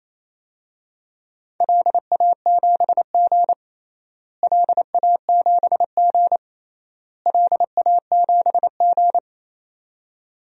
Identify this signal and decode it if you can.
Morse code, heard as a keyed tone, about 27 wpm, 700 Hz. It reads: LA7G LA7G LA7G
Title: DXCC-100 Call Signs - 4 Characters - Repeated 3x - 28wpm